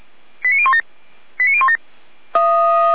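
Five-tone selective call (ZVEI), heard as quick runs of five tones, tones of about 80 ms. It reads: 89017 89017
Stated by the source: ZVEI five-tone call received by SDR, from the fire brigade Feuerwehr Orlamünde.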